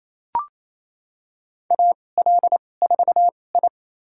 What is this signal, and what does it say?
Morse code, heard as a keyed tone, about 28 wpm, spaced as for 20 wpm, 700 Hz. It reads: AL4I
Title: USA Call Signs - 4 Characters - 2x Character Spacing - 28wpm